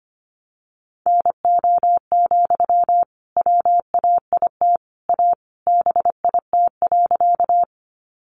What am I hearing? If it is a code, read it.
Morse code, heard as a keyed tone, about 25 wpm, 700 Hz. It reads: NO, WAIT A BIT.
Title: Sentences from Top 900 Words 25wpm